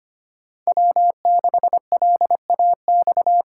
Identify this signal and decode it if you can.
Morse code, heard as a keyed tone, about 25 wpm, 700 Hz. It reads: W6LAX
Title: K1USN SST QSO Practice - 25wpm